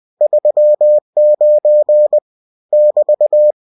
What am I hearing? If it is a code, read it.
Morse code, heard as a keyed tone, about 20 wpm, 600 Hz. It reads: 39 <BT>
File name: cw-books / winnie-the-pooh / winnie-20wpm-page-0039